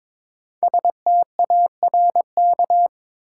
Morse code, heard as a keyed tone, about 22 wpm, 700 Hz. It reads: STARK
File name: 5 Letter Words - First 3 Letters Given - 22wpm